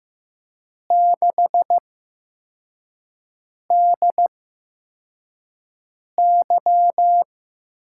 Morse code, heard as a keyed tone, about 15 wpm, spaced as for 4 wpm, 700 Hz. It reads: 6DY
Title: USA Call Signs - 8x Character Spacing - 15wpm